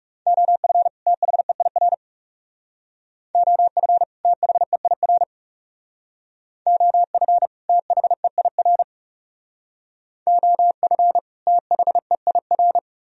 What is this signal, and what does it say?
Morse code, heard as a keyed tone, about 39 wpm, 700 Hz. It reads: OF THEIR OF THEIR OF THEIR OF THEIR